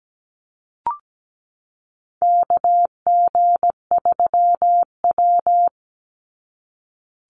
Morse code, heard as a keyed tone, about 17 wpm, 700 Hz. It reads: KG3W
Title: NA SOTA Chaser Callsigns 17wpm